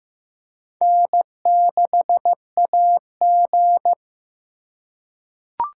Morse code, heard as a keyed tone, about 15 wpm, 700 Hz. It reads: N6AG
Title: USA Call Signs - 6x Character Spacing - 15wpm